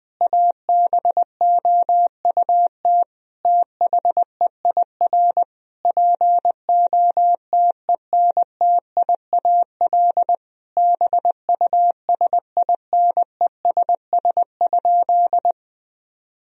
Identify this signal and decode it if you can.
Morse code, heard as a keyed tone, about 21 wpm, 700 Hz. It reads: ABOUT THEIR POTENTIAL BUSINESS?
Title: Sentences from Top 1000 Words - Review Part 4 - 20wpm